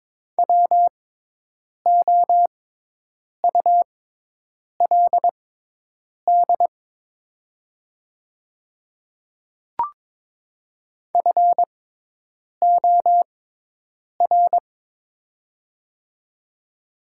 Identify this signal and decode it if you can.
Morse code, heard as a keyed tone, about 22 wpm, spaced as for 8 wpm, 690 Hz. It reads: WOULD FOR
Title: 6x Character Spacing - Top 100 Words - 22wpm